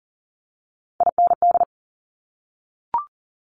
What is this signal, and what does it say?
Morse code, heard as a keyed tone, about 39 wpm, 700 Hz. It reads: IND
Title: Top Three Letter Combinations in Words 40wpm